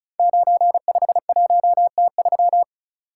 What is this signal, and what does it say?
Morse code, heard as a keyed tone, about 36 wpm, 700 Hz. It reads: 951T3
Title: Code Groups - 5 Letter:Numbers - 35wpm